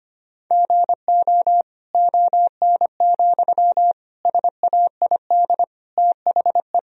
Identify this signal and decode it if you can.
Morse code, heard as a keyed tone, about 25 wpm, 700 Hz. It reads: GO ON, SAID THE
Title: Sentences from Top 1000 Words - Review Part 4 - 25wpm